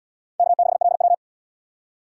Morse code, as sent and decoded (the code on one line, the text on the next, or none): -.- -.... .-.. .--
K6LW